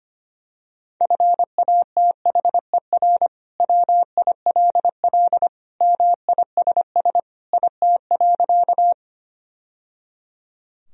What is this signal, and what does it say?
Morse code, heard as a keyed tone, about 25 wpm, 700 Hz. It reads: FATHER WILL MISS IT.
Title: Sentences from Top 1000 Words - Review Part 3 - 25wpm